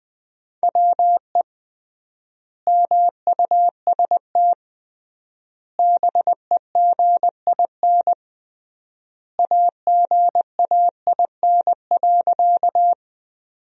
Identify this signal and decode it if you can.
Morse code, heard as a keyed tone, about 20 wpm, 700 Hz. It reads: WE MUST BEGIN AGAIN.
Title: 3x Word-Spacing - Sentences from Top 300 Words - 20wpm